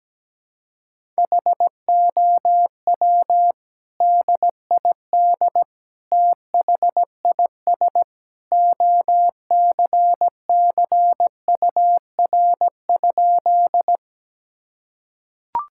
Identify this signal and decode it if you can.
Morse code, heard as a keyed tone, about 17 wpm, 700 Hz. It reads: HOW DID THIS OCCUR?
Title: Sentences from Top 1000 Words - Review Part 6 - 17wpm